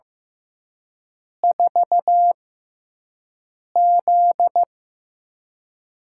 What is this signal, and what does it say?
Morse code, heard as a keyed tone, about 15 wpm, spaced as for 5 wpm, 700 Hz. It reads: 4Z
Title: USA Call Signs - 4 Characters - 6x Character Spacing - 15wpm